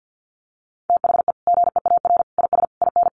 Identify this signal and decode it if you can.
Morse code, heard as a keyed tone, about 50 wpm, 700 Hz. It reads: THE GEAR IS IN
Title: Sentences from Top 1400 Words 50wpm